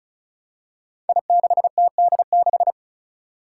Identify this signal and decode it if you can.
Morse code, heard as a keyed tone, about 36 wpm, 700 Hz. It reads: I6TD6
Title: Five Challenging Letter-Number Combinations 35wpm